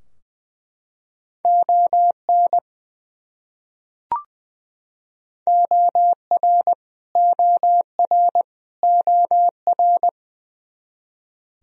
Morse code, heard as a keyed tone, about 20 wpm, 700 Hz. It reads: ON OR OR OR
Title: Top 100 Words Repeated 20wpm